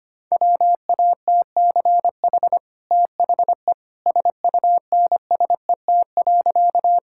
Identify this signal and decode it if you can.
Morse code, heard as a keyed tone, about 25 wpm, 690 Hz. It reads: WATCH THE SUNSET.
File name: Easy Sentences with Hints - 25wpm